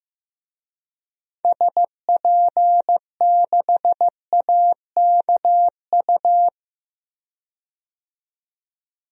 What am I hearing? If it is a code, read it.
Morse code, heard as a keyed tone, about 15 wpm, 700 Hz. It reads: SP6AKU
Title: DXCC-100 Call Signs - 6 Characters - 15wpm